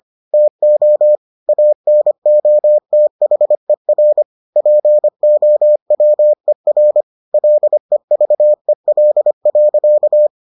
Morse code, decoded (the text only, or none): TO ANOTHER POWER LEVEL.